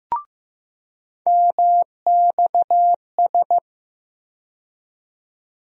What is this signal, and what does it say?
Morse code, heard as a keyed tone, about 15 wpm, 700 Hz. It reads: MXS
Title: USA Call Sign Postfixes - 3 Letters 15wpm